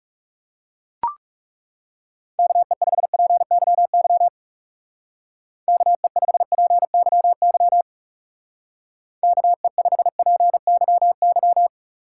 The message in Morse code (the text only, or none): KE5PYY KE5PYY KE5PYY